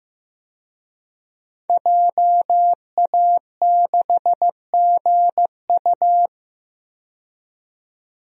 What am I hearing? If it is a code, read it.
Morse code, heard as a keyed tone, about 15 wpm, 700 Hz. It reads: JA6GU
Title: DXCC-100 Call Signs - 5 Characters - 15wpm